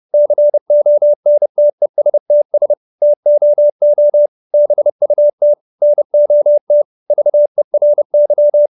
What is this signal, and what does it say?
Morse code, heard as a keyed tone, about 30 wpm, 590 Hz. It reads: CONTESTS TOO BUT NOT VERY
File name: QSO315 30